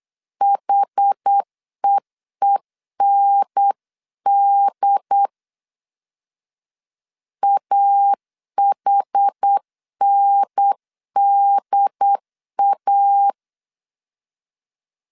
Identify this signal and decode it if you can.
Morse code, heard as a keyed tone, about 8 wpm, 800 Hz. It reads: HEEND AHNDA